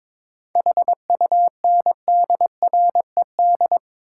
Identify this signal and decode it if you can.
Morse code, heard as a keyed tone, about 22 wpm, 700 Hz. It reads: HUNDRED
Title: Sentences from Top 400 Words - New Word Drill - 22wpm